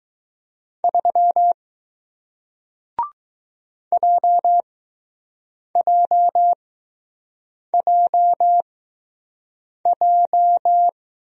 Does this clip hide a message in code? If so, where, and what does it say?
Yes: Morse code from 0.8 to 10.9 s, heard as a keyed tone, about 20 wpm, spaced as for 6 wpm, 700 Hz. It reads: 3 JJJJ